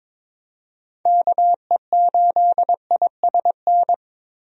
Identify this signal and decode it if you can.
Morse code, heard as a keyed tone, about 22 wpm, 700 Hz. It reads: KE8ISN
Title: USA Call Signs - 6 Characters - Repeated 2x - 22wpm